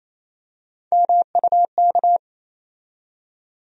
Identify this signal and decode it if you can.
Morse code, heard as a keyed tone, about 28 wpm, 700 Hz. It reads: MUK
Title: USA Call Sign Postfixes - 3 Letters 28wpm